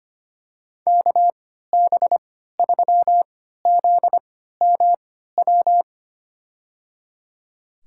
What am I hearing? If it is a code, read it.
Morse code, heard as a keyed tone, about 25 wpm, spaced as for 14 wpm, 700 Hz. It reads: KB3ZMW
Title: USA Call Signs - 6 Characters - 3x Character Spacing - 25wpm